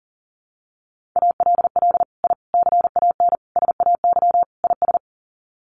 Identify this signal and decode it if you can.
Morse code, heard as a keyed tone, about 40 wpm, 700 Hz. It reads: ALL I CAN SAY IS
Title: Sets of 5 Words Encore2 40wpm